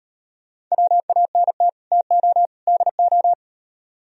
Morse code, heard as a keyed tone, about 38 wpm, 700 Hz. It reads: WANT TO DO